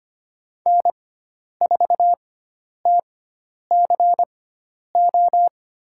Morse code, heard as a keyed tone, about 25 wpm, spaced as for 10 wpm, 700 Hz. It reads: N4TCO
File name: Lesson 16g - Letter c - Call Signs 25wpm